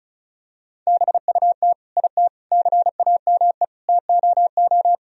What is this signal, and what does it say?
Morse code, heard as a keyed tone, about 36 wpm, 700 Hz. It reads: BUT IT CAME TOO
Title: Sentences from Top 1000 Words - Review Part 2 - 35wpm